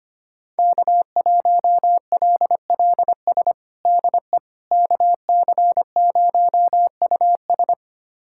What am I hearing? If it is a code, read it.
Morse code, heard as a keyed tone, about 25 wpm, 700 Hz. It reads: K1LLS DE KC0US
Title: USA Callsign DE Callsign - 5 Characters - 25wpm